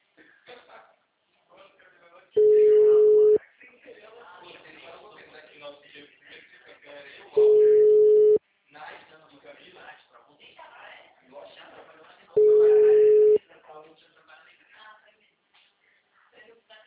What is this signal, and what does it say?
Ringback tone (Europe, 425 Hz), with speech.